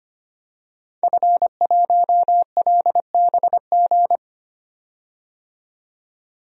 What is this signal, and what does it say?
Morse code, heard as a keyed tone, about 25 wpm, 700 Hz. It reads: F1LBG
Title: DXCC-100 Call Signs - 5 Characters - 25wpm